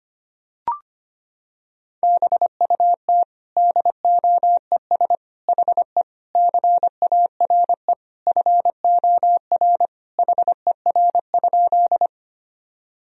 Morse code, heard as a keyed tone, about 25 wpm, 700 Hz. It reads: BUT DOES HE CARE FOR HER?
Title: Sentences from Top 1000 Words - Review Part 1 - 25wpm